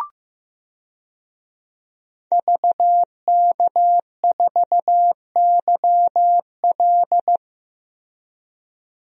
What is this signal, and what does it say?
Morse code, heard as a keyed tone, about 15 wpm, 700 Hz. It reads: VK4YL
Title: DXCC-100 Call Signs - 5 Characters - 15wpm